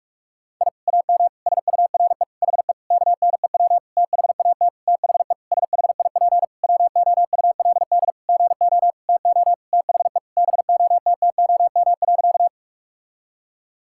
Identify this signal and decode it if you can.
Morse code, heard as a keyed tone, about 46 wpm, 700 Hz. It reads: I AM SURE HE KNEW THAT THE SHIP WOULD GO TO THE BOTTOM.
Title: Sentences from Top 1000 Words - Review Part 6 - 45wpm